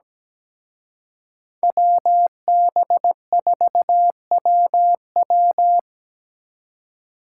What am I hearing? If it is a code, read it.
Morse code, heard as a keyed tone, about 20 wpm, 700 Hz. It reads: WB4WW
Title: Speed Racing - USA 2x2 Call Signs 17wpm